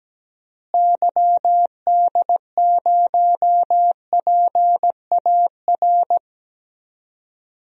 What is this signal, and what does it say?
Morse code, heard as a keyed tone, about 17 wpm, 700 Hz. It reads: YD0PAR
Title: DXCC-100 Call Signs - 6 Characters - Repeated 2x - 17wpm